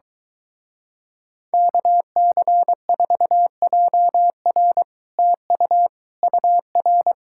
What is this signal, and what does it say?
Morse code, heard as a keyed tone, about 23 wpm, 700 Hz. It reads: KC4JR TU UR